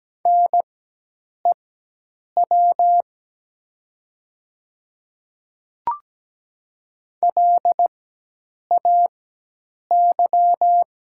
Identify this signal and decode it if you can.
Morse code, heard as a keyed tone, about 17 wpm, spaced as for 8 wpm, 700 Hz. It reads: NEW LAY
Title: 4x Character Spacing - Top 500 Words - 17wpm